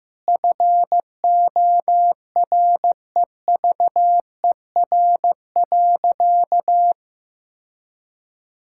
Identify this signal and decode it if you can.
Morse code, heard as a keyed tone, about 15 wpm, 700 Hz. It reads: FOREVER.
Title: Sentences from Top 1600 Words 15wpm